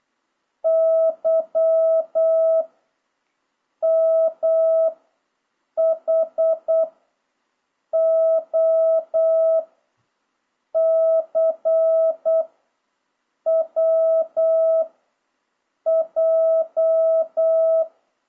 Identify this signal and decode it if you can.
Morse code, heard as a keyed tone, about 8 wpm, 660 Hz. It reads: Y M H O C W J
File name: jh8cbh / 2026cw